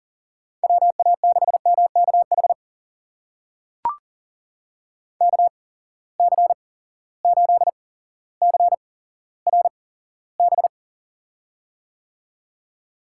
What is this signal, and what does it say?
Morse code, heard as a keyed tone, about 40 wpm, spaced as for 17 wpm, 700 Hz. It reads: WA6MKH KC8CRB
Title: USA Call Signs - 6 Characters - 8x Character Spacing - 40wpm